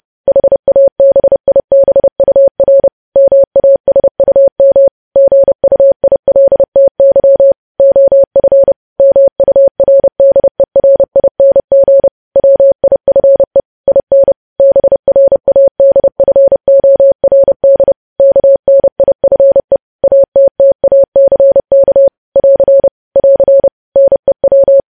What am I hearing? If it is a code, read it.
Morse code, heard as a keyed tone, about 30 wpm, 550 Hz. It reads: HABIBUR MASUM GUILTY OF MURDERING WIFE IN BRADFORD KNIFE ATTACK <AR> <AR> NEW
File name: podcast / News headlines-1x-30x30